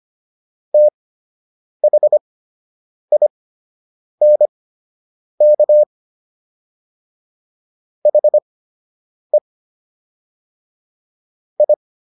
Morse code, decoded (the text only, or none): THINK HE I